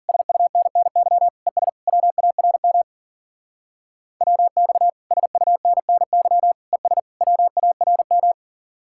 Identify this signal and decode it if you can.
Morse code, heard as a keyed tone, about 50 wpm, 700 Hz. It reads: SUNNY ES WARM WX SUNNY ES WARM
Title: Speed-Racing Common QSO Phrases Encore - 40wpm